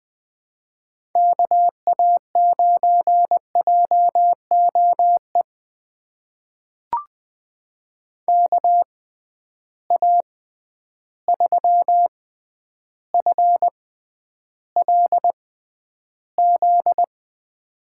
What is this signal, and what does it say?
Morse code, heard as a keyed tone, about 20 wpm, spaced as for 10 wpm, 700 Hz. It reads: KA9JOE KA3FLZ
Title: USA Call Signs - 6 Characters - 6x Character Spacing - 20wpm